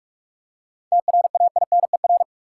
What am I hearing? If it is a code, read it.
Morse code, heard as a keyed tone, about 46 wpm, 700 Hz. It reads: TRAINER
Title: Seven Letter Words - 45wpm